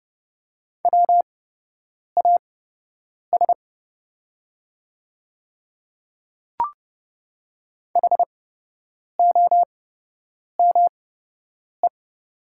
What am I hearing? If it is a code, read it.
Morse code, heard as a keyed tone, about 30 wpm, spaced as for 8 wpm, 700 Hz. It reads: WAS HOME
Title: 8x Character Spacing - Top 200 Words - 30wpm